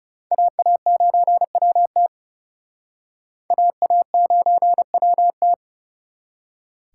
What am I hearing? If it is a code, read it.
Morse code, heard as a keyed tone, about 40 wpm, 700 Hz. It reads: AA9WT AA9WT